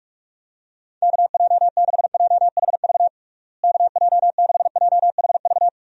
Morse code, heard as a keyed tone, about 46 wpm, 700 Hz. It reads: KJ6JHV KJ6JHV
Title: USA Call Signs - 6 Characters - Repeated 2x - 45wpm